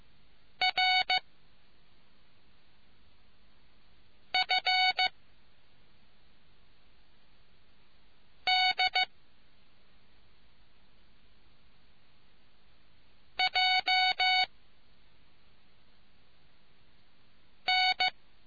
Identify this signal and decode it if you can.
Morse code, heard as a keyed tone, about 15 wpm, 740 Hz. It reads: R F D J N